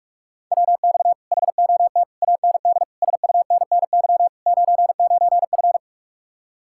Morse code, heard as a keyed tone, about 49 wpm, 700 Hz. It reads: WX HOT AND SUNNY 99F